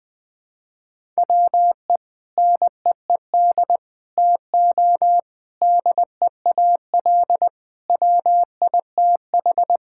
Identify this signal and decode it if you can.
Morse code, heard as a keyed tone, about 20 wpm, 700 Hz. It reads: WE NEED TO DEAL WITH
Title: Sentences with 4-Word Phrases - Spoken Beginning - 20wpm